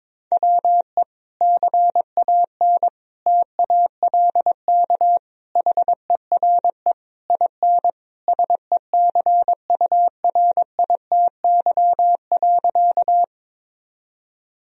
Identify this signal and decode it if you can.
Morse code, heard as a keyed tone, about 22 wpm, 700 Hz. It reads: WE CAN TALK HERE IN SECURITY.
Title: Sentences from Top 1000 Words - Review Part 4 - 22wpm